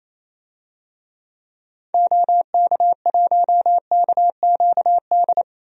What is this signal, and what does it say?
Morse code, heard as a keyed tone, about 28 wpm, 700 Hz. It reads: OK1KQD
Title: DXCC-100 Call Signs - 6 Characters - 28wpm